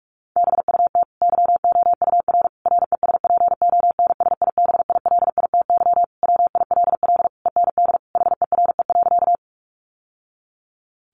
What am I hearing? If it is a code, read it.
Morse code, heard as a keyed tone, about 44 wpm, 700 Hz. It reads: BUT YOUR RESPONSIBILITY WILL END HERE.
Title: Sentences from Top 1200 Words 45wpm